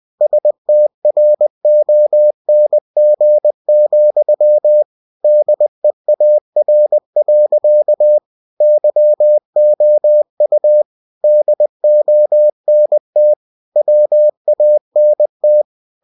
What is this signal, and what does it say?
Morse code, heard as a keyed tone, about 20 wpm, 600 Hz. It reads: STRONG, DEAR. YOU DONT WANT
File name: cw-books / winnie-the-pooh / winnie-20wpm-page-0073